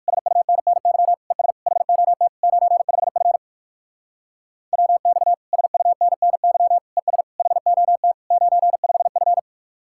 Morse code, heard as a keyed tone, about 58 wpm, 690 Hz. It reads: SUNNY ES HOT 95F WX SUNNY ES HOT 95F